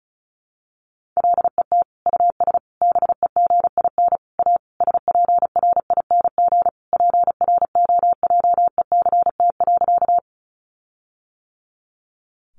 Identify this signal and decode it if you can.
Morse code, heard as a keyed tone, about 34 wpm, 700 Hz. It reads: LET US BEGIN A SPRING PROJECT.
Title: Sentences from Top 1400 Words 35wpm